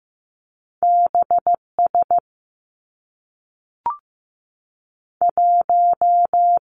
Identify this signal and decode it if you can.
Morse code, heard as a keyed tone, about 15 wpm, 700 Hz. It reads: B S 1
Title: Two Challenging Letter-Number Combinations 15wpm